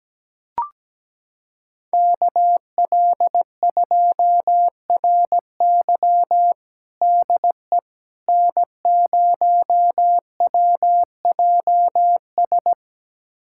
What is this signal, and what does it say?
Morse code, heard as a keyed tone, about 17 wpm, 700 Hz. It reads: KL2RY DE N0WJS
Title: USA Callsign DE Callsign - 5 Characters - 17wpm